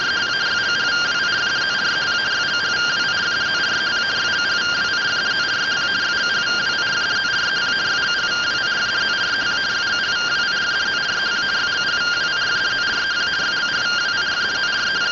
Two-tone teleprinter signal, about 170 Hz shift.